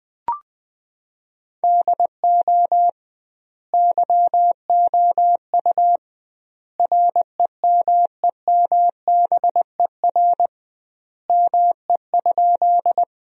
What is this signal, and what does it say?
Morse code, heard as a keyed tone, about 20 wpm, 700 Hz. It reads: DO YOU REMEMBER ME?
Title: 2x Word-Spacing - Sentences from Top 400 Words - 20wpm